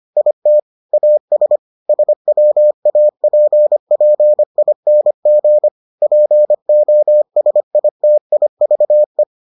Morse code, heard as a keyed tone, about 25 wpm, 600 Hz. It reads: IT AS SWAPPING POSITIVE